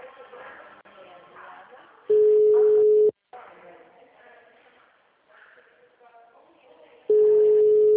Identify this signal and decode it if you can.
Ringback tone (Europe, 425 Hz), with speech.
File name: out- -2003- - - .60319